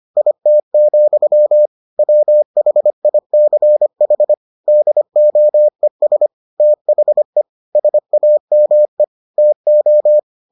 Morse code, heard as a keyed tone, about 25 wpm, 600 Hz. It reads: IT, WHICH DOES THE SAME TO